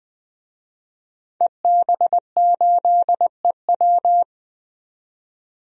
Morse code, heard as a keyed tone, about 20 wpm, 700 Hz. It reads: EB8EW
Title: DXCC-100 Call Signs - 5 Characters - 20wpm